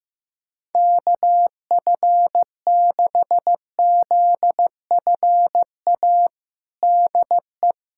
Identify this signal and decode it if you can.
Morse code, heard as a keyed tone, about 15 wpm, 700 Hz. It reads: KF6ZFA DE
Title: USA Callsign DE Callsign - 6 Characters - 15wpm